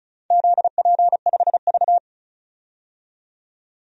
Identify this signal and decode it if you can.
Morse code, heard as a keyed tone, about 34 wpm, 700 Hz. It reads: ZP5V
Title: DXCC-100 Call Signs - 4 Characters - 35wpm